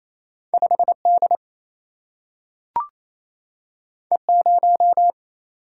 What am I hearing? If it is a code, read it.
Morse code, heard as a keyed tone, about 28 wpm, 700 Hz. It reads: 5 D E 0
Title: Single-Letter Single-Number 28wpm